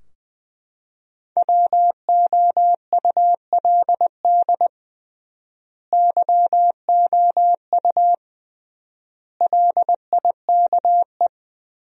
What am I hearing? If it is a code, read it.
Morse code, heard as a keyed tone, about 20 wpm, 700 Hz. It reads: WOULD YOU LIKE
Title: 3x Word-Spacing - Sentences from Top 100 Words - 20wpm